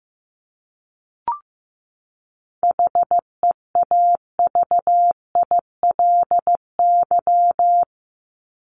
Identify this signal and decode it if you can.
Morse code, heard as a keyed tone, about 15 wpm, 700 Hz. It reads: HEAVILY
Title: LY Suffix - Top 100 Words with Base Words 15wpm